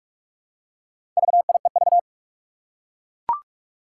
Morse code, heard as a keyed tone, about 44 wpm, 700 Hz. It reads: VIEV
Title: Four Challenging Letter-Number Combinations 45wpm